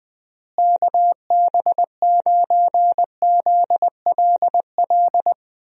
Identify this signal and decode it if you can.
Morse code, heard as a keyed tone, about 20 wpm, 700 Hz. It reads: KB9ZLL
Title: USA Call Signs - 6 Characters - Repeated 3x - 20wpm